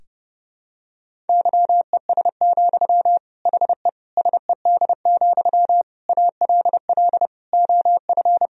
Morse code, heard as a keyed tone, about 32 wpm, 700 Hz. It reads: YES, HE SED, ALL OF
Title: Sentences from Top 500 Words and Abbreviations 30wpm